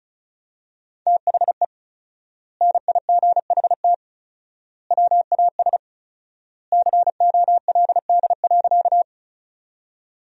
Morse code, heard as a keyed tone, about 36 wpm, 700 Hz. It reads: THE NIGHT WAS COLD.
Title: 4x Word-Spacing - Sentences from Top 400 Words - 35wpm